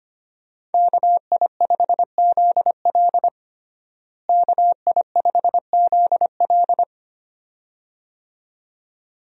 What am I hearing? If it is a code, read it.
Morse code, heard as a keyed tone, about 25 wpm, 700 Hz. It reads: KI5ZL KI5ZL
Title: USA Call Signs - 5 Characters - Repeated 2x - 25wpm